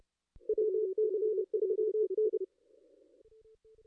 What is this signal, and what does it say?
Morse code, heard as a keyed tone, about 30 wpm, 440 Hz. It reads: WC3N
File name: contests / ss-cw-k5tr-op-n5rz -2200